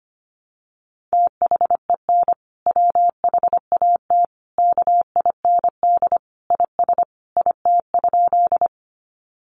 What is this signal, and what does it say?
Morse code, heard as a keyed tone, about 25 wpm, 700 Hz. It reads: THEN WHAT KIND IS IT?
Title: Sentences from Top 200 Words 25wpm